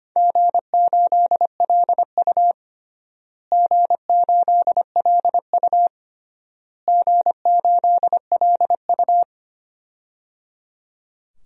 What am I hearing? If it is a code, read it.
Morse code, heard as a keyed tone, about 25 wpm, 700 Hz. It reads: G8LU G8LU G8LU
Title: DXCC-100 Call Signs - 4 Characters - Repeated 3x - 25wpm